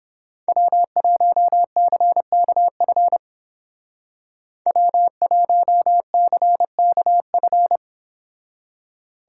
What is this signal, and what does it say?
Morse code, heard as a keyed tone, about 30 wpm, 700 Hz. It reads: W1CKF W1CKF